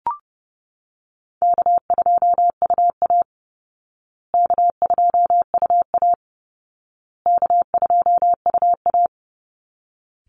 Morse code, heard as a keyed tone, about 30 wpm, 700 Hz. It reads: K2UA K2UA K2UA
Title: Easy Call Signs Repeated 3x - 30wpm